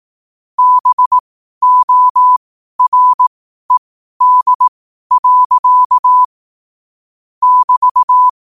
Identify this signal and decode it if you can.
Morse code, heard as a keyed tone, about 18 wpm, spaced as for 12 wpm, 1000 Hz. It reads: BORED. <BT>
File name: qotdfiles / 12wpm